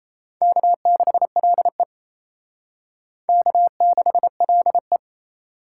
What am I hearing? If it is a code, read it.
Morse code, heard as a keyed tone, about 35 wpm, 700 Hz. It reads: K6LE K6LE
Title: Speed Racing - USA 1x2 Call Signs 28wpm